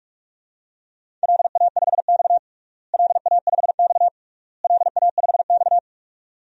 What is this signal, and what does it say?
Morse code, heard as a keyed tone, about 41 wpm, 700 Hz. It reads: LA5X LA5X LA5X
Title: DXCC-100 Call Signs - 4 Characters - Repeated 3x - 45wpm